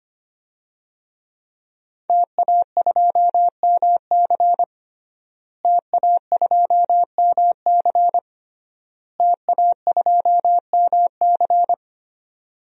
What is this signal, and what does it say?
Morse code, heard as a keyed tone, about 25 wpm, 700 Hz. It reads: TA2MC TA2MC TA2MC
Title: DXCC-100 Call Signs - 5 Characters - Repeated 3x - 25wpm